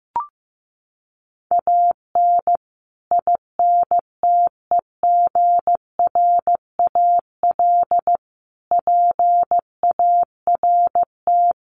Morse code, heard as a keyed tone, about 15 wpm, 700 Hz. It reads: AN INTEGRAL PART
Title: Sets of 5 Words 15wpm